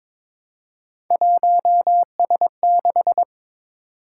Morse code, heard as a keyed tone, about 22 wpm, 700 Hz. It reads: 1S6
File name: practice / 22wpm / Three Letter-Number Combinations - 22wpm